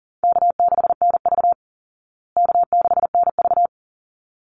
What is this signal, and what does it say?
Morse code, heard as a keyed tone, about 37 wpm, 700 Hz. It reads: K6NV K6NV
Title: Easy Call Signs Repeated 2x - 40wpm